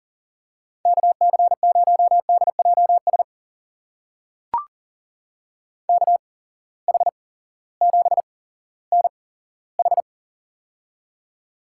Morse code, heard as a keyed tone, about 40 wpm, spaced as for 18 wpm, 700 Hz. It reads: KC0DJS KH7NH
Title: USA Call Signs - 8x Character Spacing - 40wpm